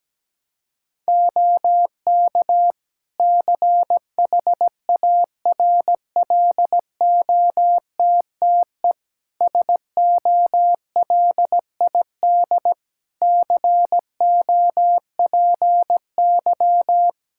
Morse code, heard as a keyed tone, about 17 wpm, 700 Hz. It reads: OK CHARLOTTE SOLID COPY
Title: Practice QSO - Exchange 3 - More Info - 17wpm